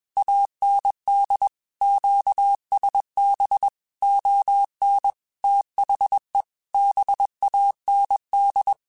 Morse code, heard as a keyed tone, about 21 wpm, 800 Hz. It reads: AND QSB ON THE BAND
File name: QSOs / QSOs-mp3s / WD0BGP 1@20